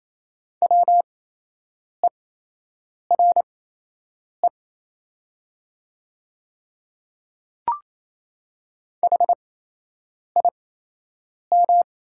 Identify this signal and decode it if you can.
Morse code, heard as a keyed tone, about 28 wpm, spaced as for 8 wpm, 700 Hz. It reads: WERE HIM
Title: 8x Character Spacing - Top 500 Words - 28wpm